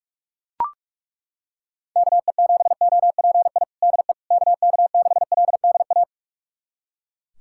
Morse code, heard as a keyed tone, about 41 wpm, 700 Hz. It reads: KE7OPI DE KK6LDA
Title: USA Callsign DE Callsign - 6 Characters - 45wpm